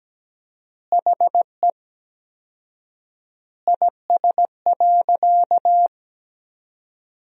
Morse code, heard as a keyed tone, about 17 wpm, 700 Hz. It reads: HE IS.
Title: 4x Word-Spacing - Sentences from Top 200 Words - 17wpm